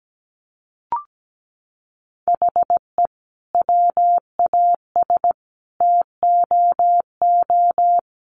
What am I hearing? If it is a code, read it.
Morse code, heard as a keyed tone, about 17 wpm, 700 Hz. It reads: HE WAS TOO
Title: Sentences from Top 600 Words 17wpm